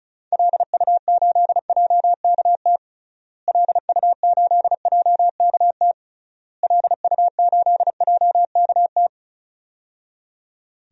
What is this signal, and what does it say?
Morse code, heard as a keyed tone, about 36 wpm, 700 Hz. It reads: LU8JKT LU8JKT LU8JKT
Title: DXCC-100 Call Signs - 6 Characters - Repeated 3x - 35wpm